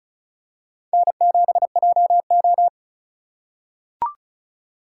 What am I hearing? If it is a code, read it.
Morse code, heard as a keyed tone, about 34 wpm, 700 Hz. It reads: N7JO
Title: USA Call Signs - 4 Characters - 3x Character Spacing - 35wpm